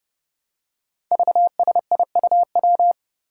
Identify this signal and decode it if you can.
Morse code, heard as a keyed tone, about 30 wpm, 700 Hz. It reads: VSIUW
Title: Five Challenging Letter-Number Combinations 30wpm